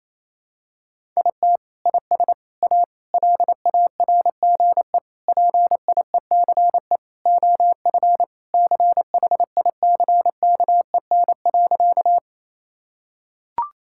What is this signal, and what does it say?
Morse code, heard as a keyed tone, about 28 wpm, 700 Hz. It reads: IT IS A LARGE PIECE OF CHICKEN.
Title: Sentences from Top 1000 Words - Review Part 5 - 28wpm